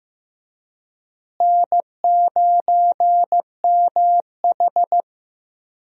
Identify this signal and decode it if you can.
Morse code, heard as a keyed tone, about 15 wpm, 700 Hz. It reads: N9MH
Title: USA Call Signs - 4 Characters - Repeated 3x - 15wpm